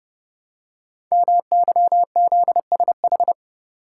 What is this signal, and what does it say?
Morse code, heard as a keyed tone, about 30 wpm, 700 Hz. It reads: MYZSH
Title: Code Groups - 5 Letter:Numbers - 30wpm